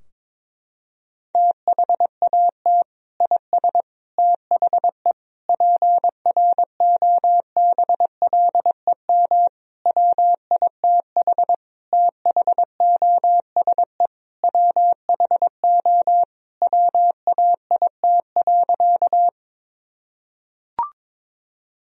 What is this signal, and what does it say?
Morse code, heard as a keyed tone, about 22 wpm, 700 Hz. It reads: THAT IS THE PROBLEM WITH THOSE WHO WAIT.
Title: Sentences from Top 1000 Words - Review Part 2 - 22wpm